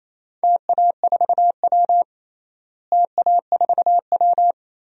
Morse code, heard as a keyed tone, about 27 wpm, 700 Hz. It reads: TA4W TA4W
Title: DXCC-100 Call Signs - 4 Characters - Repeated 3x - 28wpm